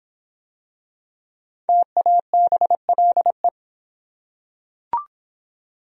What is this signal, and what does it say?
Morse code, heard as a keyed tone, about 23 wpm, 700 Hz. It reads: TABLE